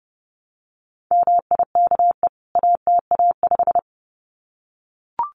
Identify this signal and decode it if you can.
Morse code, heard as a keyed tone, about 30 wpm, 700 Hz. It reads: MIKE ATA5
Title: CWT Exchange 30wpm